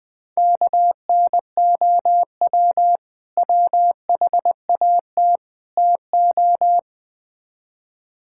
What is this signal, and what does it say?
Morse code, heard as a keyed tone, about 20 wpm, 700 Hz. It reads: KNOW WHAT TO
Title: Build A Phrase - 4-Word Phrases - 20wpm